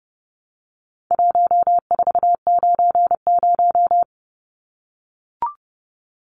Morse code, heard as a keyed tone, about 30 wpm, 700 Hz. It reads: 1490
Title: Numbers - 4-Digit 30wpm